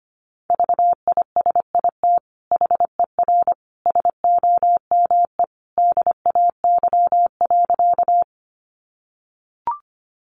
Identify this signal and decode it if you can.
Morse code, heard as a keyed tone, about 25 wpm, 700 Hz. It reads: VISIT HER SOME DAY.
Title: Sentences from Top 900 Words 25wpm